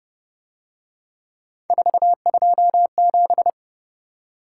Morse code, heard as a keyed tone, about 29 wpm, 700 Hz. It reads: 427
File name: Speed-Racing - Three Letter-Number Combinations - 25wpm